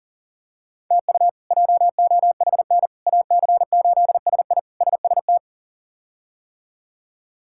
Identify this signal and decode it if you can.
Morse code, heard as a keyed tone, about 40 wpm, 700 Hz. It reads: TU JOHN AC8SI SST
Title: K1USN SST QSO Practice - 40wpm